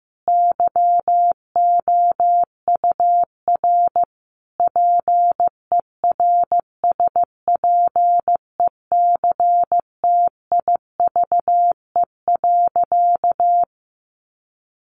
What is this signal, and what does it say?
Morse code, heard as a keyed tone, about 15 wpm, 700 Hz. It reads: YOUR PERSPECTIVE.
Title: Sentences from Top 1200 Words 15wpm